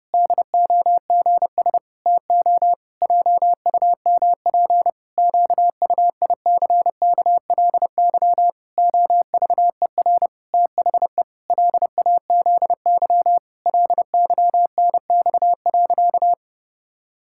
Morse code, heard as a keyed tone, about 30 wpm, 700 Hz. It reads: DOGS TO JUMP QUICKLY OVER THE LAZY LYNX.
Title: Pangram Challenge v2 - 30wpm